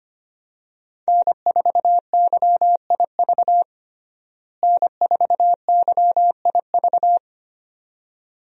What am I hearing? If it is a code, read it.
Morse code, heard as a keyed tone, about 25 wpm, 700 Hz. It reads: N4YIV N4YIV
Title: USA Call Signs - 5 Characters - Repeated 2x - 25wpm